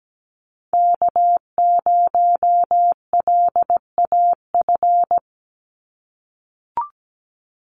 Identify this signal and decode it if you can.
Morse code, heard as a keyed tone, about 17 wpm, 700 Hz. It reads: K0LAF
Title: NA SOTA Chaser Callsigns 17wpm